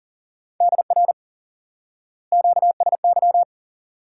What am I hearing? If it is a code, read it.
Morse code, heard as a keyed tone, about 39 wpm, 700 Hz. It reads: DR QSY
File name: IQE Course 3 - Lesson 89 - YAGI - Two Elements - 40wpm